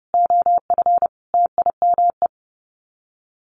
Morse code, heard as a keyed tone, about 30 wpm, 700 Hz. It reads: OF TIME
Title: Sets of 5 Words 30wpm